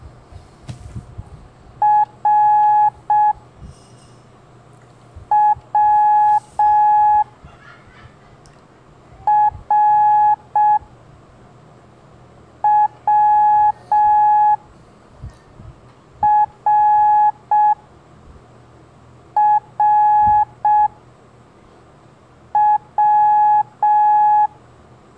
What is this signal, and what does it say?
Morse code, heard as a keyed tone, about 6 wpm, 840 Hz. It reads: R W R W R R W
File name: blog / LETRAS COM SOM SEMELHANTE MORSE